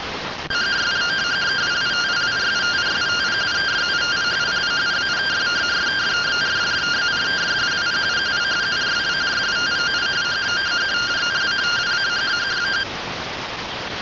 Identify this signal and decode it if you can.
Two-tone teleprinter signal, about 170 Hz shift.